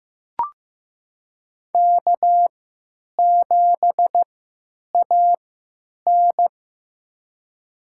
Morse code, heard as a keyed tone, about 15 wpm, spaced as for 9 wpm, 700 Hz. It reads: K7AN
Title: USA Call Signs - 4 Characters - 3x Character Spacing - 15wpm